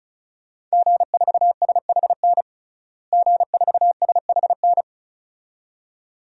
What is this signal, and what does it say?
Morse code, heard as a keyed tone, about 34 wpm, 700 Hz. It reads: G4SHN G4SHN
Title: DXCC-100 Call Signs - 5 Characters - Repeated 2x - 35wpm